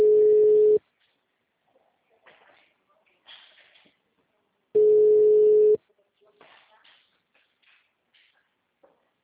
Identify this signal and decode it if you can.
Ringback tone (Europe, 425 Hz), with speech.
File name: out- -2012- - - .0